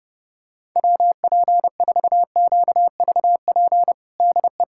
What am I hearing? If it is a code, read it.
Morse code, heard as a keyed tone, about 30 wpm, 700 Hz. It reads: WP4QVP DE
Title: USA Callsign DE Callsign 30wpm